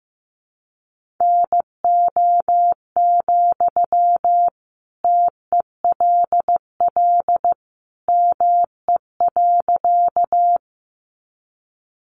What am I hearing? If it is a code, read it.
Morse code, heard as a keyed tone, about 15 wpm, 700 Hz. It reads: NO, TELL ME.
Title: Sentences from Top 200 Words 15wpm